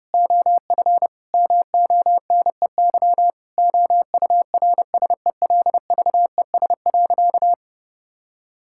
Morse code, heard as a keyed tone, about 32 wpm, 700 Hz. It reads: OF MONEY OURSELVES.
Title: Sentences from Top 1600 Words 30wpm